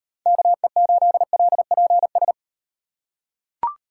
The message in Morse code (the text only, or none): KE8LPS